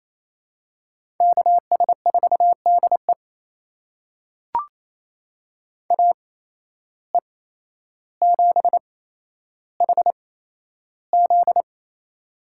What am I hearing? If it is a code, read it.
Morse code, heard as a keyed tone, about 28 wpm, spaced as for 12 wpm, 700 Hz. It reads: KS4DE AE7HZ
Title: USA Call Signs - 5 Characters - 8x Character Spacing - 28wpm